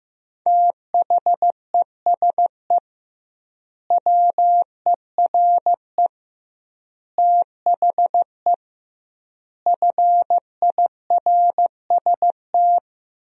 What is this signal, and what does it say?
Morse code, heard as a keyed tone, about 15 wpm, 700 Hz. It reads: THESE WERE THE FIRST
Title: 2x Word-Spacing - Sentences from Top 300 Words - 15wpm